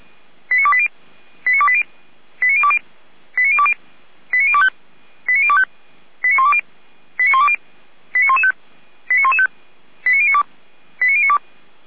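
Five-tone selective call (ZVEI), heard as quick runs of five tones, tones of about 70 ms. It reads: 89280 89280 89020 89020 89025 89025 89120 89120 89115 89115 89092 89092
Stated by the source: ZVEI five-tone call received by SDR, from the feuerwehr-orlamuende.org fire brigade audio server.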